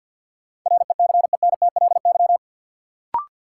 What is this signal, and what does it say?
Morse code, heard as a keyed tone, about 49 wpm, 700 Hz. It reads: RECENTLY
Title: Sentences from Top 1200 Words - New Word Drill Alt - 50wpm